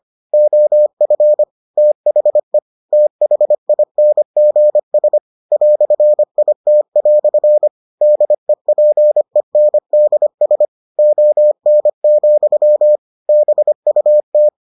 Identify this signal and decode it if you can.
Morse code, heard as a keyed tone, about 25 wpm, 600 Hz. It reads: OF THE THINGS "IT" DEPENDS ON, BUT